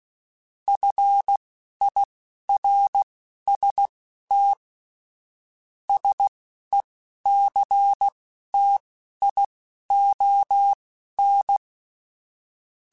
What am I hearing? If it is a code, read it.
Morse code, heard as a keyed tone, about 16 wpm, spaced as for 11 wpm, 800 Hz. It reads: FIRST SECTION
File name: Technical Subjects / Technical Subjects-mp3s / Sw Pwr Sup @10